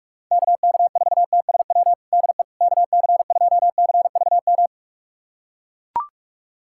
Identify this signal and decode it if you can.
Morse code, heard as a keyed tone, about 41 wpm, 700 Hz. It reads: KK4TSW DE KC2CVM
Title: USA Callsign DE Callsign - 6 Characters - 45wpm